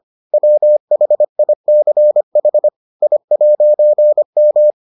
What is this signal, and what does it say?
Morse code, heard as a keyed tone, about 25 wpm, 600 Hz. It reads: WHICH I'M